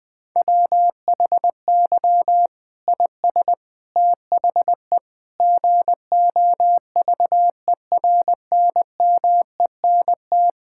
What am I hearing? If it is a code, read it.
Morse code, heard as a keyed tone, about 21 wpm, 700 Hz. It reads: WHY IS THE GOVERNMENT
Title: Sentences from Top 1000 Words - Review Part 5 - 20wpm